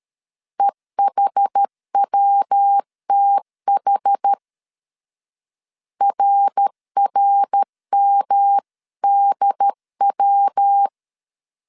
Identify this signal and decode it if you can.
Morse code, heard as a keyed tone, about 13 wpm, 800 Hz. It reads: EHWTH RRMDW
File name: CW / DET / 1 3 1500C S B 12wpm 2828